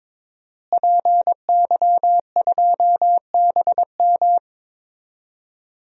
Morse code, heard as a keyed tone, about 22 wpm, 700 Hz. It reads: PY2BM
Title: DXCC-100 Call Signs - 5 Characters - Repeated 3x - 22wpm